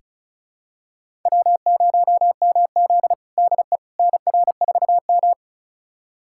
Morse code, heard as a keyed tone, about 36 wpm, 700 Hz. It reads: W0MZ DE NR4M
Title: USA Callsign DE Callsign - 4 Characters - 35wpm